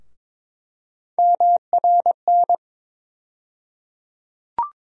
Morse code, heard as a keyed tone, about 22 wpm, 700 Hz. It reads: MRN
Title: USA Call Sign Postfixes - 3 Letters 22wpm